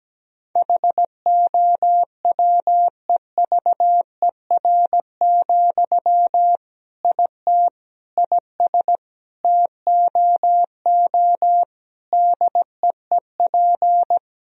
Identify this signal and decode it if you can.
Morse code, heard as a keyed tone, about 17 wpm, 700 Hz. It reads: HOWEVER, IT IS TOO DEEP
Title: Sentences from Top 1000 Words - Review Part 3 - 17wpm